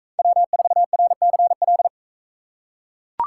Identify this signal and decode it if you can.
Morse code, heard as a keyed tone, about 42 wpm, 700 Hz. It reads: W4RCL